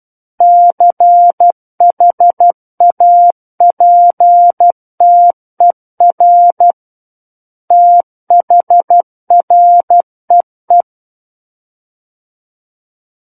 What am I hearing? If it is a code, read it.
Morse code, heard as a keyed tone, about 12 wpm, 700 Hz. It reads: CHAPTER THREE
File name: Book One / 03-Chapter 3 - 12 WPM